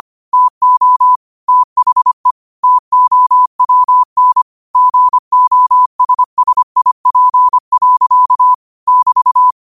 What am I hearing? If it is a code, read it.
Morse code, heard as a keyed tone, about 25 wpm, 1000 Hz. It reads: TO THE TOWN GOSSIP. <BT>